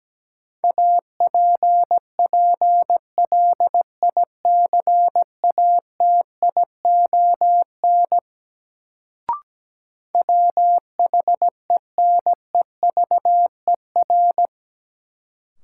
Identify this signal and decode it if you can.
Morse code, heard as a keyed tone, about 17 wpm, 700 Hz. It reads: APPLICATION WHENEVER
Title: Sentences from Top 1200 Words - New Word Drill Alt - 17wpm